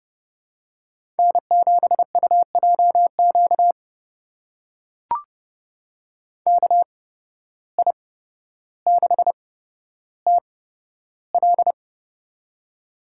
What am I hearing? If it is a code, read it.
Morse code, heard as a keyed tone, about 30 wpm, spaced as for 13 wpm, 700 Hz. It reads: N7UJQ KI6TL
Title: USA Call Signs - 5 Characters - 8x Character Spacing - 30wpm